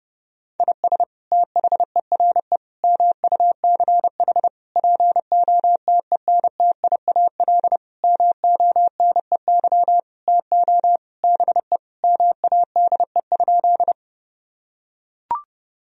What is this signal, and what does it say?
Morse code, heard as a keyed tone, about 32 wpm, 700 Hz. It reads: IS THERE MUCH POTENTIAL MONEY TO BE MADE?
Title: Sentences from Top 1000 Words - Review Part 4 - 30wpm